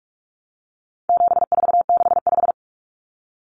Morse code, heard as a keyed tone, about 44 wpm, 700 Hz. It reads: 7465
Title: Numbers - 4-Digit 45wpm